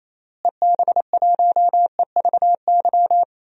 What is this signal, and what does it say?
Morse code, heard as a keyed tone, about 28 wpm, 700 Hz. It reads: EB1EVY
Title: DXCC-100 Call Signs - 6 Characters - Repeated 3x - 28wpm